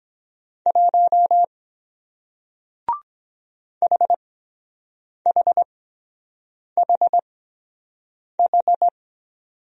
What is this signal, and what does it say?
Morse code, heard as a keyed tone, about 23 wpm, spaced as for 7 wpm, 700 Hz. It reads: 1 HHHH